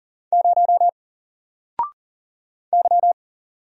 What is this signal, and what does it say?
Morse code, heard as a keyed tone, about 40 wpm, 700 Hz. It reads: 0 Y
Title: Single Letter-Number 40wpm